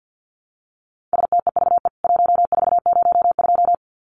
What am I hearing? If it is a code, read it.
Morse code, heard as a keyed tone, about 49 wpm, 700 Hz. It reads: STEVE 1402
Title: CWT Exchange 50wpm